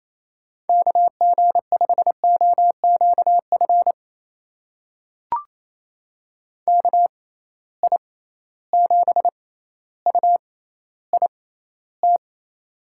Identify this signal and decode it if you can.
Morse code, heard as a keyed tone, about 28 wpm, spaced as for 14 wpm, 700 Hz. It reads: KG5OQF KI7UIT
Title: USA Call Signs - 6 Characters - 6x Character Spacing - 28wpm